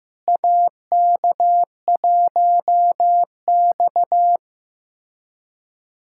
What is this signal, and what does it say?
Morse code, heard as a keyed tone, about 16 wpm, 700 Hz. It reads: AK1X
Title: USA Callsign DE Callsign - 4 Characters - 15wpm